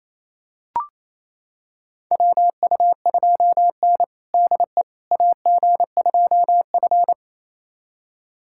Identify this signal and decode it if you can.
Morse code, heard as a keyed tone, about 28 wpm, 700 Hz. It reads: WU2N DE AG2F
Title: USA Callsign DE Callsign - 4 Characters - 28wpm